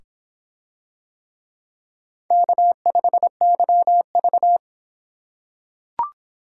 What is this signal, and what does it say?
Morse code, heard as a keyed tone, about 26 wpm, 700 Hz. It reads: K5YV